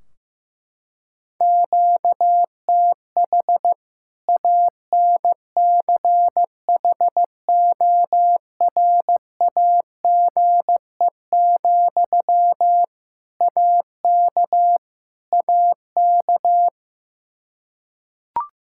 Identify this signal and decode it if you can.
Morse code, heard as a keyed tone, about 15 wpm, 700 Hz. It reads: QTH ANCHORAGE, AK AK
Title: Practice QSO - Exchange 1 - Beginning - 15wpm